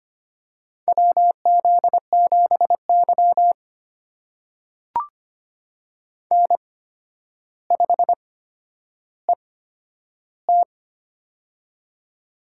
Morse code, heard as a keyed tone, about 25 wpm, spaced as for 11 wpm, 700 Hz. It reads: WZ7Y N5ET
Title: USA Call Signs - 4 Characters - 8x Character Spacing - 25wpm